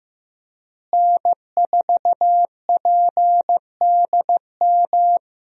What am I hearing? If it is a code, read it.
Morse code, heard as a keyed tone, about 16 wpm, 700 Hz. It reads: N4PDM
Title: USA Call Signs - 5 Characters - Repeated 2x - 15wpm